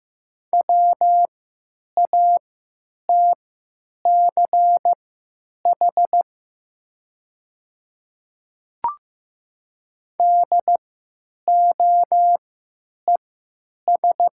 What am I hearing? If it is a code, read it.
Morse code, heard as a keyed tone, about 15 wpm, spaced as for 9 wpm, 700 Hz. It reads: WATCH DOES
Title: 3x Character Spacing - Top 500 Words - 15wpm